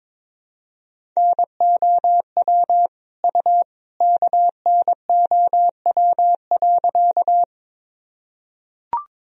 Morse code, heard as a keyed tone, about 22 wpm, 700 Hz. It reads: NOW U KNOW.
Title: Sentences from Top 100 Words and Abbreviations 22wpm